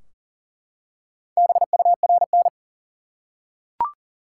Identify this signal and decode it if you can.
Morse code, heard as a keyed tone, about 40 wpm, 700 Hz. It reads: BURN
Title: Four Letter Words - 40wpm